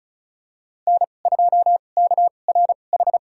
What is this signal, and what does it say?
Morse code, heard as a keyed tone, about 34 wpm, spaced as for 25 wpm, 700 Hz. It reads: N2KRH
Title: USA Call Signs - 5 Characters - 2x Character Spacing - 35wpm